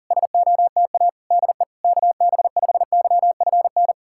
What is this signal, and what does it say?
Morse code, heard as a keyed tone, about 40 wpm, 700 Hz. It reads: SOTA DE KB5YFN
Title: SOTA QSO Practice 40wpm